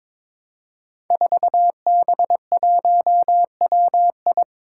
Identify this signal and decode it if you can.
Morse code, heard as a keyed tone, about 22 wpm, 700 Hz. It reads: 4B1WI
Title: Five Challenging Letter-Number Combinations 22wpm